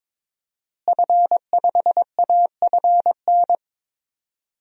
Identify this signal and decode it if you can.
Morse code, heard as a keyed tone, about 22 wpm, 700 Hz. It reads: F5AFN
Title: DXCC-100 Call Signs - 5 Characters - Repeated 3x - 22wpm